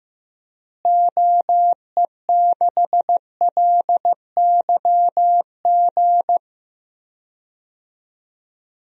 Morse code, heard as a keyed tone, about 15 wpm, 700 Hz. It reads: OE6LYG
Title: DXCC-100 Call Signs - 6 Characters - Repeated 2x - 15wpm